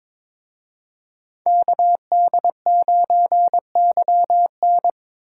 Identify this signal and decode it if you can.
Morse code, heard as a keyed tone, about 26 wpm, 700 Hz. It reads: KD9YN
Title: Speed Racing - USA Call Signs 22wpm